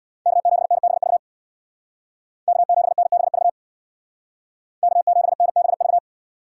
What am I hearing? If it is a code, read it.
Morse code, heard as a keyed tone, about 59 wpm, 700 Hz. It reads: K7NBV K7NBV K7NBV
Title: Speed Racing - USA 1x3 Call Signs 50wpm